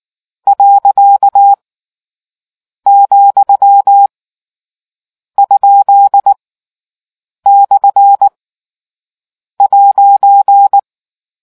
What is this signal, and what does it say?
Morse code, heard as a keyed tone, about 19 wpm, 800 Hz. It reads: . , ? / '